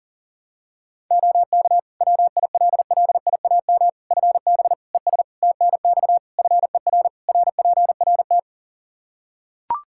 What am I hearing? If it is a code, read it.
Morse code, heard as a keyed tone, about 40 wpm, 700 Hz. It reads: OK WILLIAM FB ES TNX FER RPRT
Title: Practice QSO - Exchange 2 - Info - 40wpm